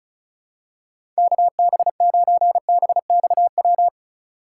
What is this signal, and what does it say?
Morse code, heard as a keyed tone, about 36 wpm, 700 Hz. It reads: KB9BXW
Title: USA Call Signs - 6 Characters - 4x Character Spacing - 35wpm